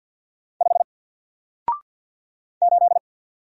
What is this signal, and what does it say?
Morse code, heard as a keyed tone, about 47 wpm, 700 Hz. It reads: 5 8
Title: Single Letter-Number 50wpm